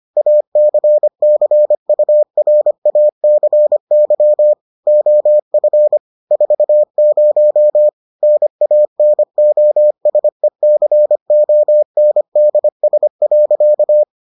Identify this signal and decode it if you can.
Morse code, heard as a keyed tone, about 25 wpm, 600 Hz. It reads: ACCURACY OF 40 NANOSECONDS.